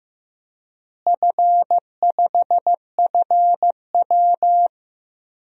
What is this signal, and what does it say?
Morse code, heard as a keyed tone, about 15 wpm, 700 Hz. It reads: F5FW
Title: DXCC-100 Call Signs - 4 Characters - Repeated 3x - 15wpm